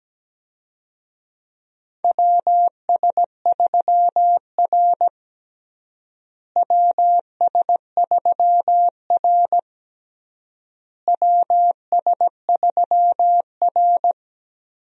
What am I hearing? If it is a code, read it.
Morse code, heard as a keyed tone, about 17 wpm, 700 Hz. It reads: WS3R WS3R WS3R
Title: USA Call Signs - 4 Characters - Repeated 3x - 17wpm